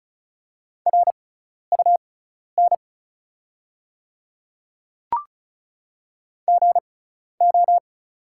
Morse code, heard as a keyed tone, about 36 wpm, spaced as for 12 wpm, 690 Hz. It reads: RUN GO
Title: 6x Character Spacing - Top 300 Words - 35wpm